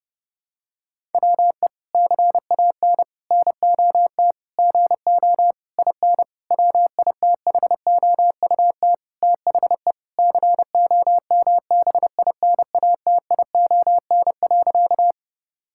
Morse code, heard as a keyed tone, about 32 wpm, 700 Hz. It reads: WE CAN NOT GO IN WITHOUT THE COMBINATION.
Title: Sentences from Top 1000 Words - Review Part 6 - 30wpm